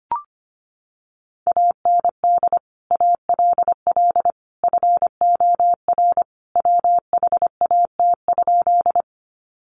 Morse code, heard as a keyed tone, about 25 wpm, 700 Hz. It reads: AND ALL FOR WHAT?
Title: Sentences from Top 100 Words 25wpm